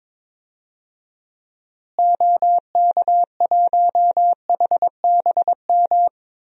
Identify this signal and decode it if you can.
Morse code, heard as a keyed tone, about 22 wpm, 700 Hz. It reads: OK1HBM
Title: DXCC-100 Call Signs - 6 Characters - Repeated 2x - 22wpm